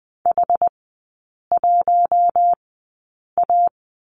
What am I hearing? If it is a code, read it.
Morse code, heard as a keyed tone, about 20 wpm, spaced as for 8 wpm, 700 Hz. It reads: H1A
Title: 087 Lesson 15e - Number 5 - Three Characters 20wpm